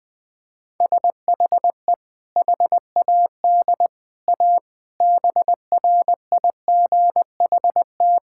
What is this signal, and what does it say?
Morse code, heard as a keyed tone, about 20 wpm, 700 Hz. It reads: SHE HAD A BRIGHT
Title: Sentences with 2-Word Phrases - Spoken Beginning - 20wpm